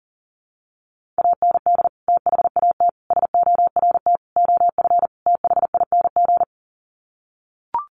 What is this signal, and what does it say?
Morse code, heard as a keyed tone, about 40 wpm, 700 Hz. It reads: AND THAT SORT OF THING
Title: Sets of 5 Words Encore2 40wpm